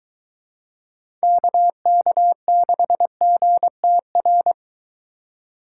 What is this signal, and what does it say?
Morse code, heard as a keyed tone, about 21 wpm, 700 Hz. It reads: KK6GTR